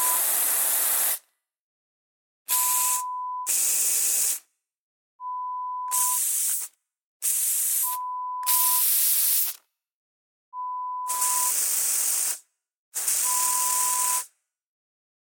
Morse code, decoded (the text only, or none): none